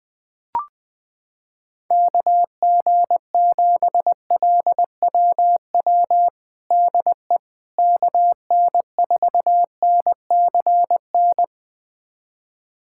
Morse code, heard as a keyed tone, about 21 wpm, 700 Hz. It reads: KG7LWW DE KN4NCN
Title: USA Callsign DE Callsign 20wpm